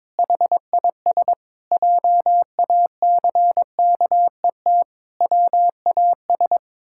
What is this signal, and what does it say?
Morse code, heard as a keyed tone, about 22 wpm, 700 Hz. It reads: HIS JACKET WAS
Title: Pangram Challenge v2 - 22wpm